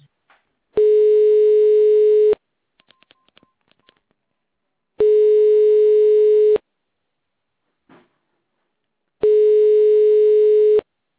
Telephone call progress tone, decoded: ringback tone